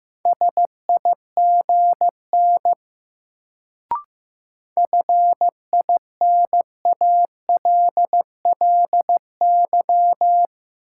Morse code, heard as a keyed tone, about 15 wpm, 700 Hz. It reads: SIGN FINALLY
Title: Sentences from Top 1000 Words - New Word Drill - 15wpm